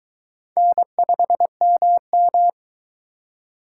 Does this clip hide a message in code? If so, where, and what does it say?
Yes: Morse code from 0.6 to 2.5 s, heard as a keyed tone, about 19 wpm, 700 Hz. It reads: N5MM